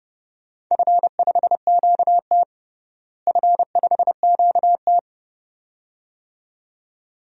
Morse code, heard as a keyed tone, about 32 wpm, 700 Hz. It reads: F5QT F5QT
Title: DXCC-100 Call Signs - 4 Characters - Repeated 2x - 30wpm